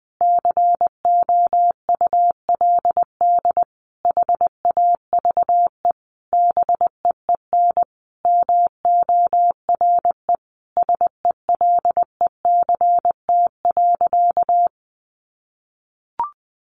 Morse code, heard as a keyed tone, about 20 wpm, 700 Hz. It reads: COULD HAVE BEEN MORE SELECT.
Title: Sentences from Top 1300 Words 20wpm